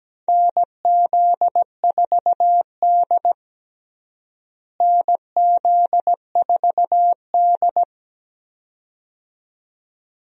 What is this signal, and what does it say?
Morse code, heard as a keyed tone, about 17 wpm, 700 Hz. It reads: NZ4D NZ4D
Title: USA Call Signs - 4 Characters - Repeated 2x - 17wpm